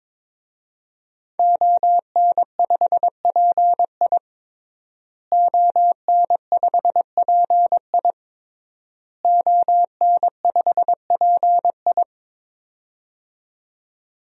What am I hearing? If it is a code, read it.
Morse code, heard as a keyed tone, about 22 wpm, 700 Hz. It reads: ON5PI ON5PI ON5PI
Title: DXCC-100 Call Signs - 5 Characters - Repeated 3x - 22wpm